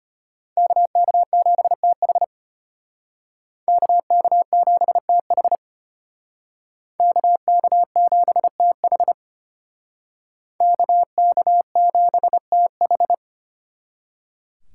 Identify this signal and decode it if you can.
Morse code, heard as a keyed tone, about 32 wpm, 700 Hz. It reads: KK7TH KK7TH KK7TH KK7TH